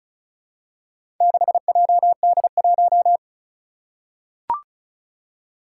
Morse code, heard as a keyed tone, about 36 wpm, 700 Hz. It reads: 6JD1
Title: Four Challenging Letter-Number Combinations 35wpm